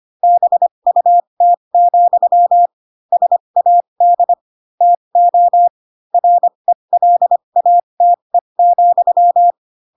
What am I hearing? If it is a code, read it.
Morse code, heard as a keyed tone, about 25 wpm, spaced as for 22 wpm, 700 Hz. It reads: BUT, SAD TO RELATE,
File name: Pinocchio / Pinocchio ch4 18 25 ik1hge